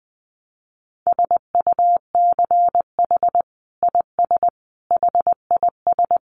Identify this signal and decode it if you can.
Morse code, heard as a keyed tone, about 19 wpm, 700 Hz. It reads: SUCH IS HIS
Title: Sentences from Top 900 Words 20wpm